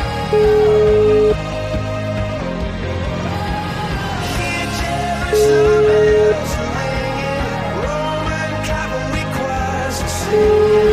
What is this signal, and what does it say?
Ringback tone (Europe, 425 Hz), with noise.